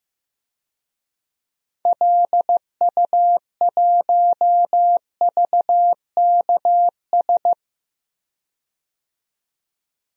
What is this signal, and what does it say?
Morse code, heard as a keyed tone, about 15 wpm, 700 Hz. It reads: LU1VKS
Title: DXCC-100 Call Signs - 6 Characters - 15wpm